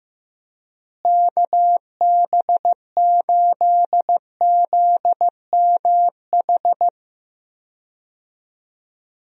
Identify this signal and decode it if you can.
Morse code, heard as a keyed tone, about 15 wpm, 700 Hz. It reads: KB8ZMH
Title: USA Call Signs - 6 Characters - Repeated 3x - 15wpm